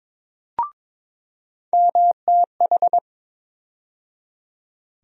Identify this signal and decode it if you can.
Morse code, heard as a keyed tone, about 22 wpm, 700 Hz. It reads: MTH
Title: USA Call Sign Postfixes - 3 Letters 22wpm